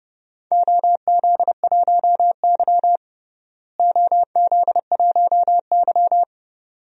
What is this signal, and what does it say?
Morse code, heard as a keyed tone, about 31 wpm, 700 Hz. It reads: OZ1Y OZ1Y
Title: DXCC-100 Call Signs - 4 Characters - Repeated 2x - 30wpm